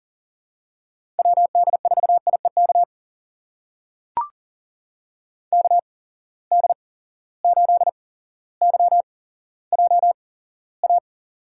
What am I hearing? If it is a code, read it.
Morse code, heard as a keyed tone, about 40 wpm, spaced as for 17 wpm, 700 Hz. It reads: WD4IEK KD8YJA
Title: USA Call Signs - 6 Characters - 8x Character Spacing - 40wpm